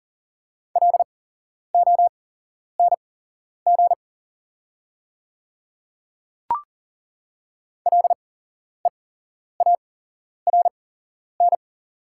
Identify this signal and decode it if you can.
Morse code, heard as a keyed tone, about 38 wpm, spaced as for 11 wpm, 690 Hz. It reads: LONG LEARN
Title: 8x Character Spacing - Top 500 Words - 40wpm